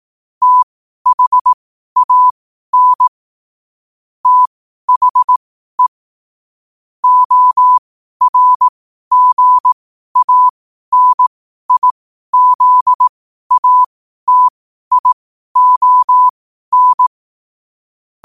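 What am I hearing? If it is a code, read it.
Morse code, heard as a keyed tone, about 18 wpm, spaced as for 12 wpm, 1000 Hz. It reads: THAN THE ORGANIZATION
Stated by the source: Morse code quotes at 12 words per minute, by ai4qr.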